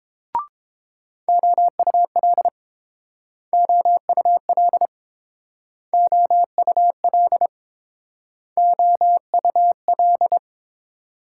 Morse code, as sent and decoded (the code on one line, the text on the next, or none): --- ..- .-.. / --- ..- .-.. / --- ..- .-.. / --- ..- .-..
OUL OUL OUL OUL